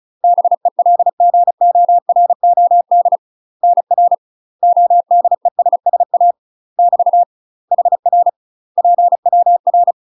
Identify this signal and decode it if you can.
Morse code, heard as a keyed tone, about 36 wpm, 700 Hz. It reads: BELGOROD NR ODESSA <BT> HR PWR
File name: UK MASTER QSOs TRAINING / 35WPM / UK MASTER QSOs 0011